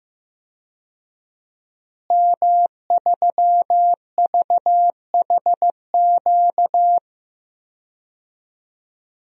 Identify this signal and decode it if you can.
Morse code, heard as a keyed tone, about 15 wpm, 700 Hz. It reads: M3VHQ
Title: DXCC-100 Call Signs - 5 Characters - Repeated 2x - 15wpm